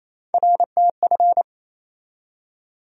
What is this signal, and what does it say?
Morse code, heard as a keyed tone, about 28 wpm, 700 Hz. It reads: RTF
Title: USA Call Sign Postfixes - 3 Letters 28wpm